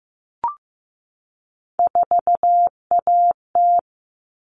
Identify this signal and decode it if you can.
Morse code, heard as a keyed tone, about 15 wpm, 700 Hz. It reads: 4AT
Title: All Cut Numbers - 3-Digit 15wpm